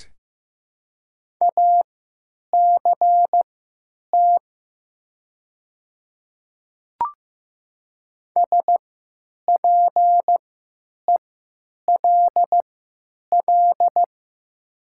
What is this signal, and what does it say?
Morse code, heard as a keyed tone, about 15 wpm, spaced as for 9 wpm, 700 Hz. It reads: ACT SPELL
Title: 3x Character Spacing - Top 200 Words - 15wpm